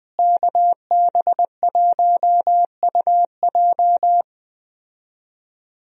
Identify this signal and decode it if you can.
Morse code, heard as a keyed tone, about 21 wpm, 700 Hz. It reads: KB1UJ
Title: USA Call Signs - 5 Characters - Repeated 3x - 20wpm